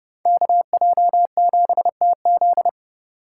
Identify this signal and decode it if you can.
Morse code, heard as a keyed tone, about 29 wpm, 700 Hz. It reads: KJ7TZ